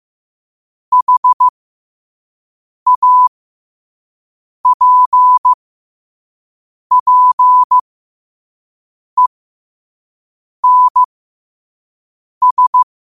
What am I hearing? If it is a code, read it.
Morse code, heard as a keyed tone, about 15 wpm, spaced as for 5 wpm, 1000 Hz. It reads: HAPPENS
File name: qotdfiles / 5wpm